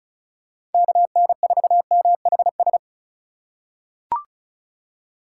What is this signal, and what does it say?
Morse code, heard as a keyed tone, about 36 wpm, 700 Hz. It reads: KN4MHS
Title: USA Call Signs - 6 Characters - 2x Character Spacing - 35wpm